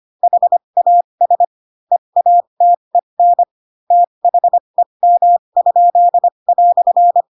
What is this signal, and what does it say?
Morse code, heard as a keyed tone, about 25 wpm, spaced as for 22 wpm, 700 Hz. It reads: HAS EATEN THEM?"
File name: Pinocchio / Pinocchio ch7 18 25 ik1hge